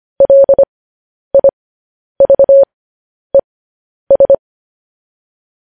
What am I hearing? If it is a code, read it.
Morse code, heard as a keyed tone, about 25 wpm, spaced as for 10 wpm, 550 Hz. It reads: LIVES
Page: news-headlines-in-morse-code-at-10-wpm